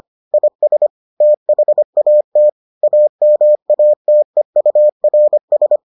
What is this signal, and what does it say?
Morse code, heard as a keyed tone, about 25 wpm, 600 Hz. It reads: IS THAT AMATEURS